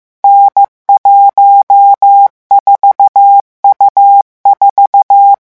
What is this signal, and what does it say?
Morse code, heard as a keyed tone, about 15 wpm, 800 Hz. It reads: N14U4
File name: morse mp3 / revision13-vitesse 14 mots